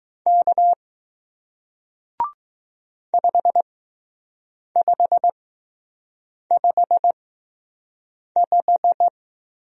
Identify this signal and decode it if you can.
Morse code, heard as a keyed tone, about 20 wpm, spaced as for 6 wpm, 700 Hz. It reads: K 5555